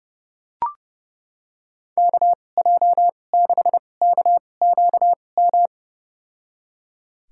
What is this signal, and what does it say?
Morse code, heard as a keyed tone, about 30 wpm, spaced as for 22 wpm, 700 Hz. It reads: KJ6KQM
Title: USA Call Signs - 6 Characters - 2x Character Spacing - 30wpm